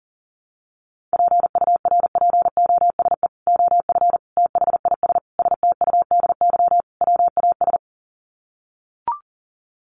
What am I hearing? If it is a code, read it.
Morse code, heard as a keyed tone, about 40 wpm, 700 Hz. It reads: PURPOSE OF THIS STUDY WAS
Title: Sets of 5 Words 40wpm